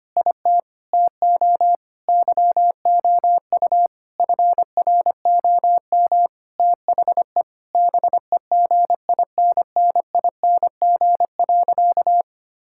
Morse code, heard as a keyed tone, about 25 wpm, 700 Hz. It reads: IT TO YOU FROM THE BEGINNING.
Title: Sentences from Top 1000 Words - Review Part 5 - 25wpm